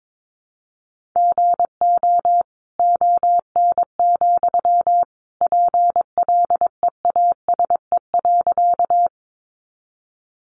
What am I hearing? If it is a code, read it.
Morse code, heard as a keyed tone, about 22 wpm, 700 Hz. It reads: GO ON, PLEASE.
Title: Sentences from Top 1500 Words 22wpm